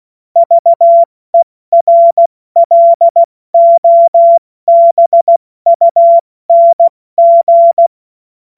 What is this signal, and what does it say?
Morse code, heard as a keyed tone, about 16 wpm, 680 Hz. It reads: VERLOBUNG
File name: Morsekurs / Witze / Witz 14. Februar 2023 70 F80